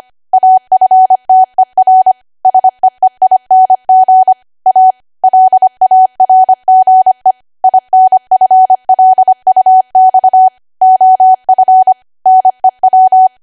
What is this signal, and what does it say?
Morse code, heard as a keyed tone, about 25 wpm, 750 Hz. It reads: AFTER SEEING A LARGE INFLUX OF NEW
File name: Archive / 25 WPM / 25WPM